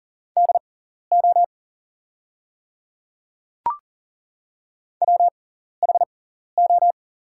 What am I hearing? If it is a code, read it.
Morse code, heard as a keyed tone, about 38 wpm, spaced as for 14 wpm, 690 Hz. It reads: DO WHO
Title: 6x Character Spacing - Top 100 Words - 40wpm